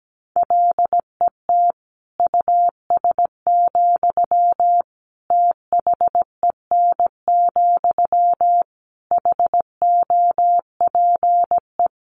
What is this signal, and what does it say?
Morse code, heard as a keyed tone, about 17 wpm, 700 Hz. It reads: LET US, THEN, HOPE
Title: Sentences from Top 1000 Words 17wpm